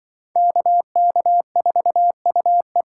Morse code, heard as a keyed tone, about 26 wpm, 700 Hz. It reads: KK4UE